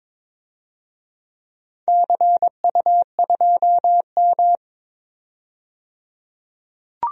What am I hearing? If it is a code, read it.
Morse code, heard as a keyed tone, about 22 wpm, 700 Hz. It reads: CU2M
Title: DXCC-100 Call Signs - 4 Characters - 22wpm